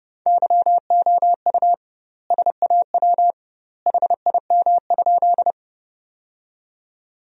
Morse code, heard as a keyed tone, about 32 wpm, 700 Hz. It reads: YOU SAW HIM?
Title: 2x Word-Spacing - Sentences from Top 300 Words - 30wpm